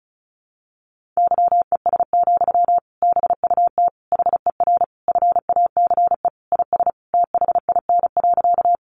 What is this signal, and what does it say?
Morse code, heard as a keyed tone, about 34 wpm, 700 Hz. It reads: YES, BUT HER FACE IS THIN.
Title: Sentences from Top 1200 Words 35wpm